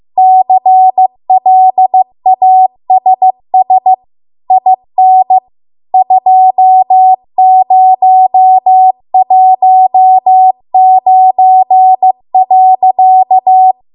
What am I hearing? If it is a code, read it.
Morse code, heard as a keyed tone, about 15 wpm, 750 Hz. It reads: CLASS IN 2019.